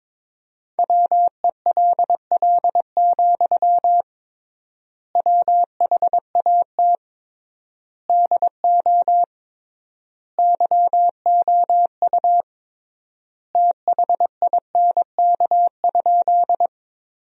Morse code, heard as a keyed tone, about 21 wpm, 700 Hz. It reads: WELL, WHAT DO YOU THINK?
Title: 3x Word-Spacing - Sentences from Top 200 Words - 22wpm